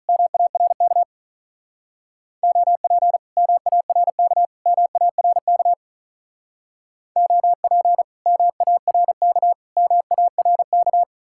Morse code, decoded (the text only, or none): MARK OP MARK MARK OP MARK MARK